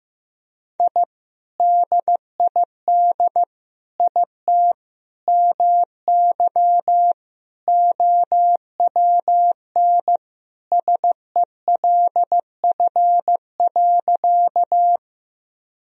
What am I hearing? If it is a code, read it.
Morse code, heard as a keyed tone, about 15 wpm, 700 Hz. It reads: I DID IT MY OWN SELF.
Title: Sentences from Top 1000 Words - Review Part 1 - 15wpm